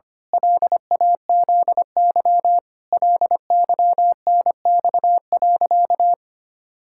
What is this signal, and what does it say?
Morse code, heard as a keyed tone, about 25 wpm, 700 Hz. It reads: LAZY LYNX.
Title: Pangram Challenge v2 - 25wpm